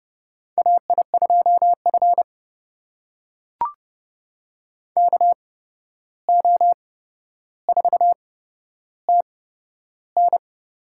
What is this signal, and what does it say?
Morse code, heard as a keyed tone, about 30 wpm, spaced as for 12 wpm, 700 Hz. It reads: AI2F KO4TN
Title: USA Call Signs - 8x Character Spacing - 30wpm